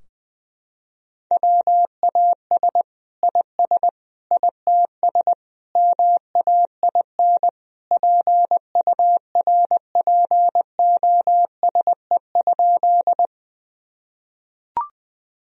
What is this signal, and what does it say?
Morse code, heard as a keyed tone, about 20 wpm, 700 Hz. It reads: WAS IS ITS MAIN PURPOSE?
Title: Sentences from Top 1000 Words - Review Part 2 - 20wpm